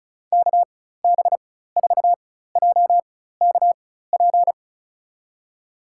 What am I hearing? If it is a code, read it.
Morse code, heard as a keyed tone, about 36 wpm, spaced as for 16 wpm, 700 Hz. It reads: KB4JKP
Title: USA Call Signs - 4x Character Spacing - 35wpm